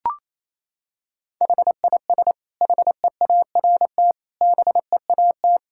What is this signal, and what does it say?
Morse code, heard as a keyed tone, about 27 wpm, 700 Hz. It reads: HIS HEART BEAT
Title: Sentences from Top 1600 Words 28wpm